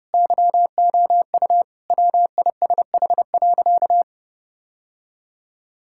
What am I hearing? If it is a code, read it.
Morse code, heard as a keyed tone, about 30 wpm, 700 Hz. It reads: YOU WISH.
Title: Sentences from Top 1600 Words 30wpm